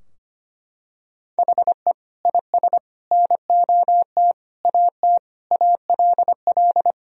Morse code, heard as a keyed tone, about 25 wpm, 700 Hz. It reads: HE IS NOT AT ALL
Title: Sentences from Top 1000 Words - Review Part 4 - 25wpm